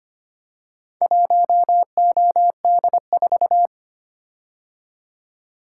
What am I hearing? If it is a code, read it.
Morse code, heard as a keyed tone, about 25 wpm, 700 Hz. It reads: 1OD4
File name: practice / 25wpm / Four Letter-Number Combinations - 25wpm